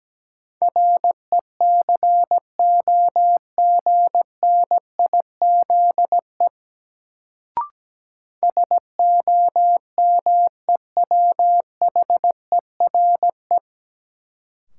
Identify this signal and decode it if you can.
Morse code, heard as a keyed tone, about 17 wpm, 700 Hz. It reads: RECOGNIZE SOMEWHERE
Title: Sentences from Top 1400 Words - New Word Drill - 17wpm